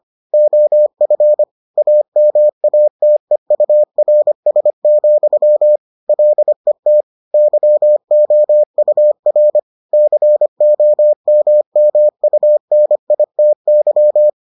Morse code, heard as a keyed tone, about 25 wpm, 600 Hz. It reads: OF AMATEURS, LET YOUR COMMUNITY